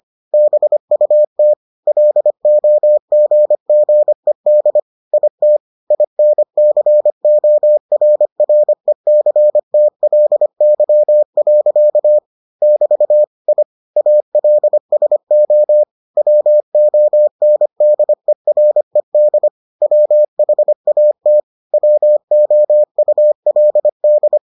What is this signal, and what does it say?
Morse code, heard as a keyed tone, about 25 wpm, 600 Hz. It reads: BUT LOGGED IT INCORRECTLY. <BT> I ALSO WONDERED WHAT WOULD